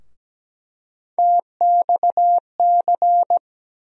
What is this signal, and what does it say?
Morse code, heard as a keyed tone, about 17 wpm, 700 Hz. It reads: TXC
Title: USA Call Sign Postfixes - 3 Letters 17wpm